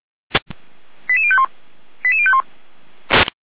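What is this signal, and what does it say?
Five-tone selective call (ZVEI), heard as quick runs of five tones, tones of about 70 ms. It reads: 88051 88051
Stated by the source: ZVEI five-tone selective call received by SDR.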